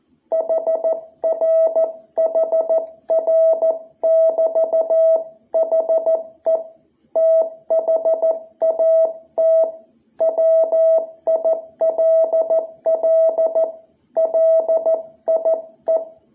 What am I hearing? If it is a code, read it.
Morse code, heard as a keyed tone, about 14 wpm, 660 Hz. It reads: HR HR <BT> HE THAT WILL LIE
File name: jh8cbh / 2021cw